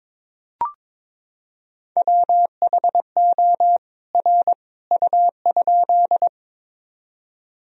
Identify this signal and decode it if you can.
Morse code, heard as a keyed tone, about 22 wpm, 700 Hz. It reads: WHO R U?
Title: Sentences from Top 100 Words and Abbreviations 22wpm